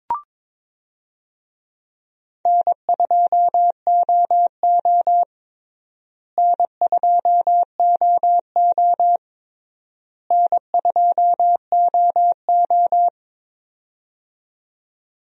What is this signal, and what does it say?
Morse code, heard as a keyed tone, about 22 wpm, 700 Hz. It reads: N2OO N2OO N2OO
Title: USA Call Signs - 4 Characters - Repeated 3x - 22wpm